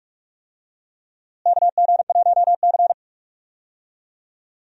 Morse code, heard as a keyed tone, about 47 wpm, 700 Hz. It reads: KG1C
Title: USA Call Signs - 4 Characters - Repeated 3x - 45wpm